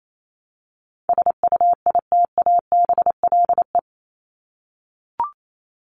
Morse code, heard as a keyed tone, about 28 wpm, 700 Hz. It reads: SUITABLE
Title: Top 1251-1500 Words 28wpm